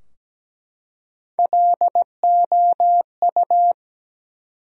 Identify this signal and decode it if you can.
Morse code, heard as a keyed tone, about 17 wpm, 700 Hz. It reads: LOU
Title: USA Common Names - Max 3 Letters - 17wpm